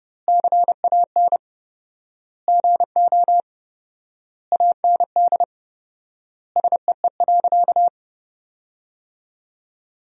Morse code, heard as a keyed tone, about 32 wpm, 700 Hz. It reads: CAN GO AND SEE.
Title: 4x Word-Spacing - Sentences from Top 100 Words - 30wpm